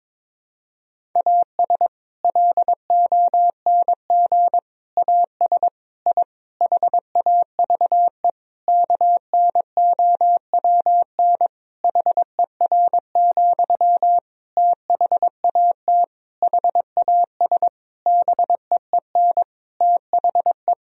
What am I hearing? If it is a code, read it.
Morse code, heard as a keyed tone, about 22 wpm, 700 Hz. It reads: AS LONG AS I HAVE KNOWN HER, THAT HAS BEEN THE
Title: Sentences from Top 1000 Words - Review Part 3 - 22wpm